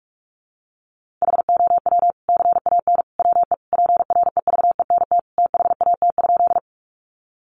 Morse code, heard as a keyed tone, about 44 wpm, 700 Hz. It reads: HOW CAN WE PREVENT THAT?
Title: Sentences from Top 1100 Words 45wpm